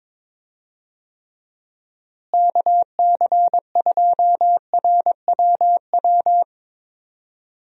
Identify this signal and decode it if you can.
Morse code, heard as a keyed tone, about 22 wpm, 700 Hz. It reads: KC2RWW
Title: USA Call Signs - 6 Characters - Repeated 2x - 22wpm